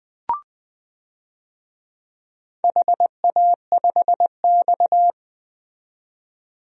Morse code, heard as a keyed tone, about 20 wpm, 700 Hz. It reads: HA5X
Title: DXCC-100 Call Signs - 4 Characters - 20wpm